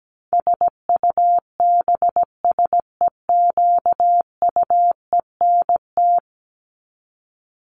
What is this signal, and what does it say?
Morse code, heard as a keyed tone, about 17 wpm, 700 Hz. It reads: SUBSEQUENT
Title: LY Suffix - Top 100 Words with Base Words 17wpm